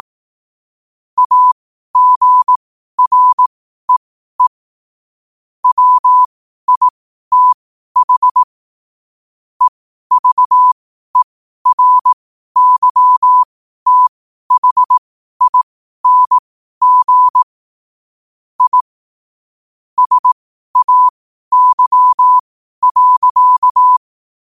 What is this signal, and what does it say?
Morse code, heard as a keyed tone, about 18 wpm, spaced as for 12 wpm, 1000 Hz. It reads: AGREE WITH EVERYTHING I SAY.
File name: 12wpm